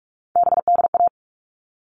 Morse code, heard as a keyed tone, about 44 wpm, 700 Hz. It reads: BDA
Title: All Cut Numbers - 3-Digit 45wpm